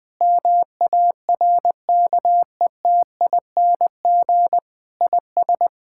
Morse code, heard as a keyed tone, about 20 wpm, 700 Hz. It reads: MARKETING IS
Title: Sentences from Top 1000 Words - Review Part 5 - 20wpm